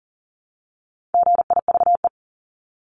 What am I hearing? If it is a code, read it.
Morse code, heard as a keyed tone, about 37 wpm, 700 Hz. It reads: GIVE
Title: Multi-letter Phonemes Repeated 3x 40wpm